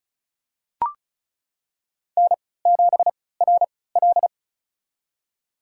Morse code, heard as a keyed tone, about 36 wpm, spaced as for 20 wpm, 700 Hz. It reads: N7RL
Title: USA Call Signs - 4 Characters - 3x Character Spacing - 35wpm